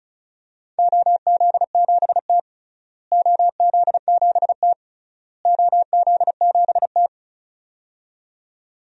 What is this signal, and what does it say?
Morse code, heard as a keyed tone, about 36 wpm, 700 Hz. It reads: OZ7T OZ7T OZ7T
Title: DXCC-100 Call Signs - 4 Characters - Repeated 3x - 35wpm